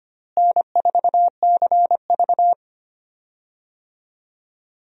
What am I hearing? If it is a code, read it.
Morse code, heard as a keyed tone, about 25 wpm, 700 Hz. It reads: N4CV
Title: USA Call Signs - 4 Characters - Repeated 3x - 25wpm